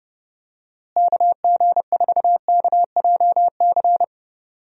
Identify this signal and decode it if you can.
Morse code, heard as a keyed tone, about 30 wpm, 700 Hz. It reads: KG4KJC
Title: USA Call Signs - 6 Characters - Repeated 2x - 30wpm